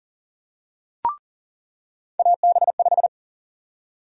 Morse code, heard as a keyed tone, about 40 wpm, 700 Hz. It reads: AB5
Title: USA Call Sign Prefixes - 40wpm